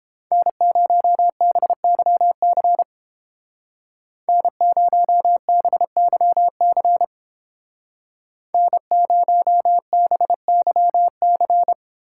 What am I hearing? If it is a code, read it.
Morse code, heard as a keyed tone, about 29 wpm, 700 Hz. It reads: N0BYC N0BYC N0BYC